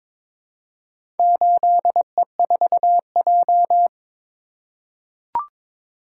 Morse code, heard as a keyed tone, about 22 wpm, 700 Hz. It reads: 8E4J
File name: practice / 22wpm / Four Letter-Number Combinations - 22wpm